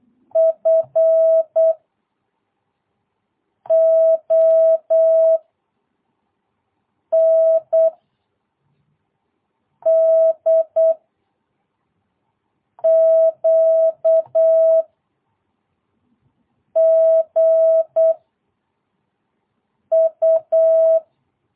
Morse code, heard as a keyed tone, about 8 wpm, 660 Hz. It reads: F O N D Q G U
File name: jh8cbh / 2022cw